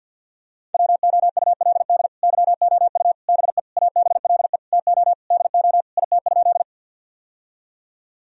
Morse code, heard as a keyed tone, about 50 wpm, 700 Hz. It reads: WOULD YOU BE ABLE TO DO IT?
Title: Sentences from Top 1000 Words - Review Part 2 - 50wpm